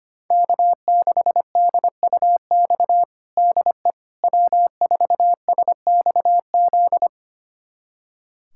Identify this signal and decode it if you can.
Morse code, heard as a keyed tone, about 25 wpm, 700 Hz. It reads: K6DUX DE W4SXZ
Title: USA Callsign DE Callsign - 5 Characters - 25wpm